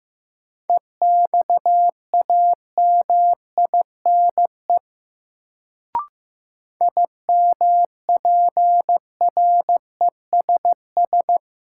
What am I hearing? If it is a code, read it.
Morse code, heard as a keyed tone, about 15 wpm, 700 Hz. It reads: EXAMINE IMPRESS
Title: Sentences from Top 1600 Words - New Word Drill Alt - 15wpm